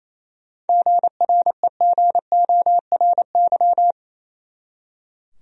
Morse code, heard as a keyed tone, about 28 wpm, 700 Hz. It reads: GREGORY
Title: USA Common Names - 28wpm